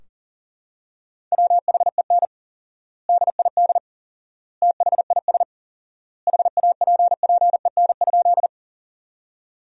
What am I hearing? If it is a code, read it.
Morse code, heard as a keyed tone, about 37 wpm, 700 Hz. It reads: WHEN DID THIS HAPPEN?
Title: 4x Word-Spacing - Sentences from Top 400 Words - 40wpm